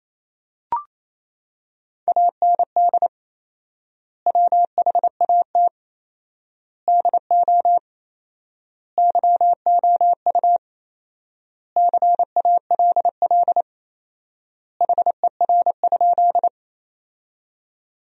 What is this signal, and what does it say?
Morse code, heard as a keyed tone, about 27 wpm, 700 Hz. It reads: AND WHAT DO YOU CALL HER?
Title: 4x Word-Spacing - Sentences from Top 100 Words - 28wpm